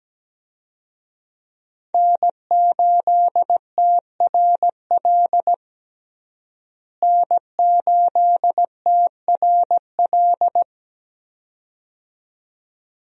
Morse code, heard as a keyed tone, about 17 wpm, 700 Hz. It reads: N8TRL N8TRL
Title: USA Call Signs - 5 Characters - Repeated 2x - 17wpm